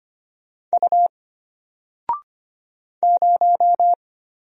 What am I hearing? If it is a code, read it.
Morse code, heard as a keyed tone, about 25 wpm, 700 Hz. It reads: U 0
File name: Single-Letter-Number-25wpm